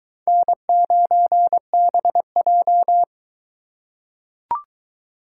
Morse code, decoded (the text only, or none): N9BJ